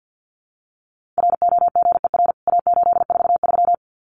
Morse code, heard as a keyed tone, about 50 wpm, 700 Hz. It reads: ROGER A843
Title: CWT Exchange 50wpm